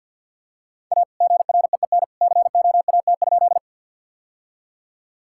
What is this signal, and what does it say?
Morse code, heard as a keyed tone, about 50 wpm, 700 Hz. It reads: A GREEN COAT?
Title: Sentences from Top 1600 Words 50wpm